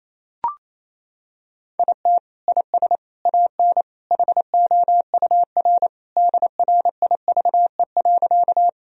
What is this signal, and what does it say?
Morse code, heard as a keyed tone, about 28 wpm, 700 Hz. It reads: IT IS AN HOUR DRIVE.
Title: Sentences from Top 1000 Words - Review Part 2 - 28wpm